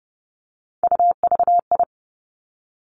Morse code, heard as a keyed tone, about 30 wpm, 700 Hz. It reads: UVI
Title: Three Challenging Letter-Number Combinations 30wpm